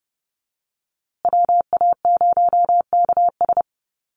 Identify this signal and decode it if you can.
Morse code, heard as a keyed tone, about 29 wpm, 700 Hz. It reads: WA0KS
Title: USA Call Signs Repeated 3x 30wpm